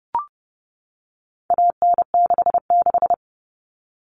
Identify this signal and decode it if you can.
Morse code, heard as a keyed tone, about 30 wpm, 700 Hz. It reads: AN66
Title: All Cut Numbers - 4-Digit 30wpm